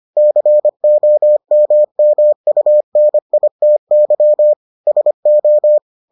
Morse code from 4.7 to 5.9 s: SO

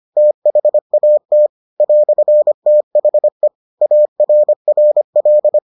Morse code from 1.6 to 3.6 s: "THE

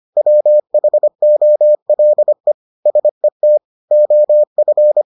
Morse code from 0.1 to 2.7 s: WHOLE

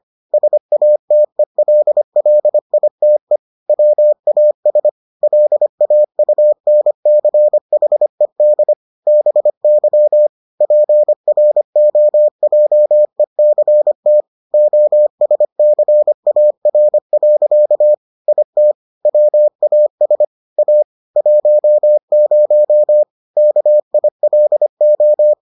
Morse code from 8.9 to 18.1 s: BY PROJECT OSCAR.

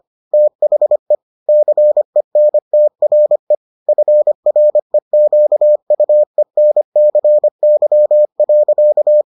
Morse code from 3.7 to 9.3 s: FREQUENCY.

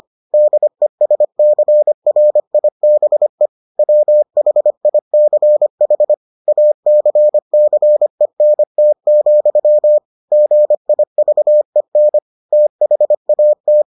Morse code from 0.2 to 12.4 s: DESCRIBE WHICH ACCENT, GIVEN

